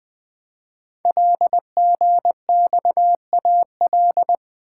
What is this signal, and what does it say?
Morse code, heard as a keyed tone, about 20 wpm, 700 Hz. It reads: LGXAL
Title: Code Groups - 5 Letters - 20wpm